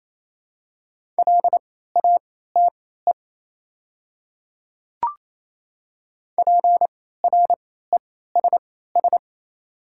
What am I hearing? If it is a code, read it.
Morse code, heard as a keyed tone, about 28 wpm, spaced as for 16 wpm, 700 Hz. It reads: LATE PRESS
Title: 3x Character Spacing - Top 300 Words - 28wpm